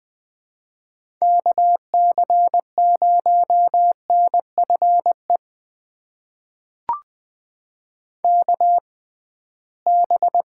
Morse code, heard as a keyed tone, about 20 wpm, spaced as for 15 wpm, 700 Hz. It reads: KC0NFE KB
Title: USA Call Signs - 6 Characters - 6x Character Spacing - 20wpm